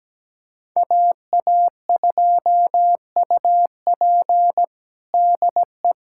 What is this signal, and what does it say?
Morse code, heard as a keyed tone, about 17 wpm, 700 Hz. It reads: AA2UP DE
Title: USA Callsign DE Callsign - 5 Characters - 17wpm